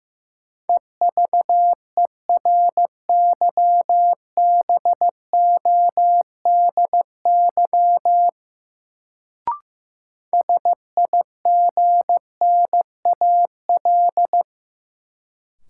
Morse code, heard as a keyed tone, about 15 wpm, 700 Hz. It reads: EVERYBODY SIGNAL
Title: Sentences from Top 1400 Words - New Word Drill Alt - 15wpm